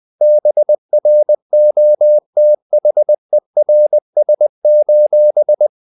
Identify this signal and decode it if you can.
Morse code, heard as a keyed tone, about 20 wpm, 600 Hz. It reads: BROTHERS:
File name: the old man and the sea / the old man and the sea part 1 20wpm